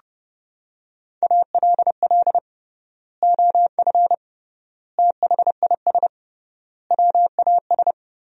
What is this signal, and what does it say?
Morse code, heard as a keyed tone, about 30 wpm, 700 Hz. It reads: ALL OF THIS WAS
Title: 3x Word-Spacing - Sentences from Top 500 Words - 30wpm